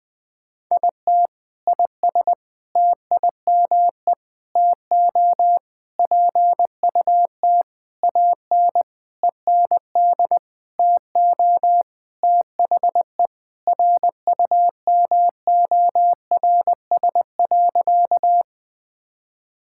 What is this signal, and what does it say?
Morse code, heard as a keyed tone, about 20 wpm, 700 Hz. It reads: IT IS TIME TO PUT AN END TO THE RUMORS.
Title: Sentences with 6-Word Phrases - Spoken Beginning - 20wpm